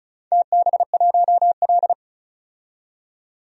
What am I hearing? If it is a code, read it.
Morse code, heard as a keyed tone, about 36 wpm, 700 Hz. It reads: TB1L
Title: DXCC-100 Call Signs - 4 Characters - 35wpm